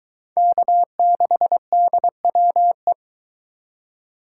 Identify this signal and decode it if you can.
Morse code, heard as a keyed tone, about 22 wpm, 700 Hz. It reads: K6DWE